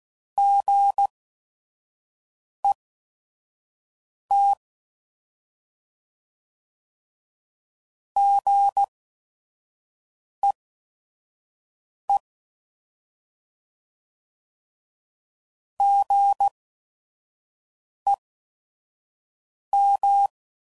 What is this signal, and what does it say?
Morse code, heard as a keyed tone, about 16 wpm, spaced as for 5 wpm, 800 Hz. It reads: GET GEE GEM
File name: Prefixes / By First Letter / Gs